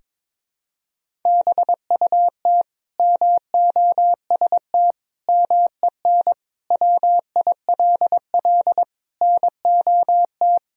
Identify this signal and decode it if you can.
Morse code, heard as a keyed tone, about 21 wpm, 700 Hz. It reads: BUT MOST MEN WILL NOT
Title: Sentences from Top 1000 Words - Review Part 1 - 22wpm